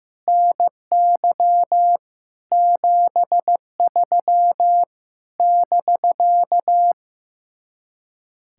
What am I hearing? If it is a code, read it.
Morse code, heard as a keyed tone, about 15 wpm, 700 Hz. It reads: NY 73 <BK>
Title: Speed-Racing Common QSO Phrases Encore - 15wpm